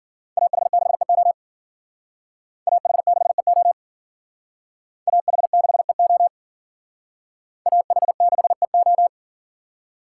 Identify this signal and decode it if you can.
Morse code, heard as a keyed tone, about 53 wpm, 700 Hz. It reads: AH6EO AH6EO AH6EO AH6EO